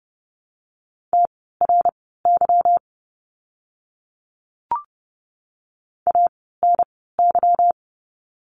Morse code, heard as a keyed tone, about 30 wpm, spaced as for 17 wpm, 700 Hz. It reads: TRY ANY
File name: course / 30:10wpm / 230 Lesson 37f - Letter x - Words 30wpm